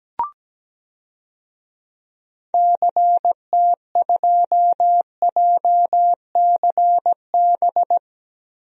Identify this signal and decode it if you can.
Morse code, heard as a keyed tone, about 17 wpm, 700 Hz. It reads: CT2JCB
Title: DXCC-100 Call Signs - 6 Characters - Repeated 2x - 17wpm